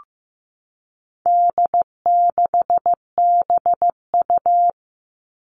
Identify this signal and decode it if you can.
Morse code, heard as a keyed tone, about 15 wpm, 700 Hz. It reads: D6BU
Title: All Cut Numbers - 4-Digit 15wpm